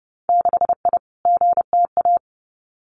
Morse code, heard as a keyed tone, about 30 wpm, 700 Hz. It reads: 6I GTA
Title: Winter Field Day Exchanges Encore 30wpm